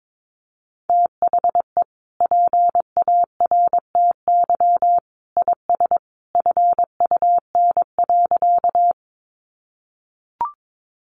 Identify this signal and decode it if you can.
Morse code, heard as a keyed tone, about 22 wpm, 700 Hz. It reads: THE PARTY IS FUN.
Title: Easy Phrases 22wpm